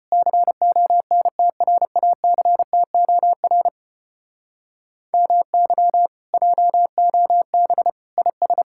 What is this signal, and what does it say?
Morse code, heard as a keyed tone, about 33 wpm, 700 Hz. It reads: CONTRACTOR MY JOB IS